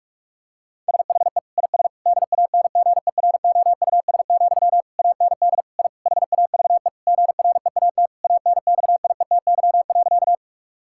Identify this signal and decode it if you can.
Morse code, heard as a keyed tone, about 44 wpm, 700 Hz. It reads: SHE IS DANGEROUS, AND I HAVE GREAT ANXIETY.
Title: Sentences from Top 1600 Words 45wpm